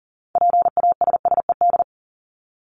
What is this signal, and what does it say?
Morse code, heard as a keyed tone, about 40 wpm, 700 Hz. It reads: PASSED
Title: ED Suffix - Top 100 Words with Base Words 40wpm